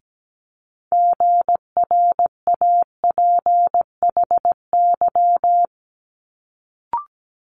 Morse code, heard as a keyed tone, about 17 wpm, 700 Hz. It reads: GRAPHY
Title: Multi-letter Phonemes Quick Fire 17wpm